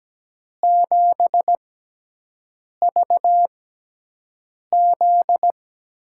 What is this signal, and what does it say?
Morse code, heard as a keyed tone, about 17 wpm, spaced as for 6 wpm, 700 Hz. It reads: 7VZ
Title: USA Call Signs - 6 Characters - 6x Character Spacing - 17wpm